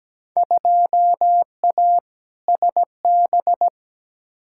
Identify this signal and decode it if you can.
Morse code, heard as a keyed tone, about 17 wpm, 700 Hz. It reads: 2A SB
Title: Field Day Exchanges - 17wpm